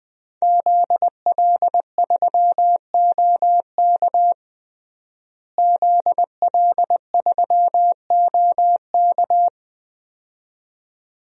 Morse code, heard as a keyed tone, about 20 wpm, 700 Hz. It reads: ZL3OK ZL3OK
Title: DXCC-100 Call Signs - 5 Characters - Repeated 2x - 20wpm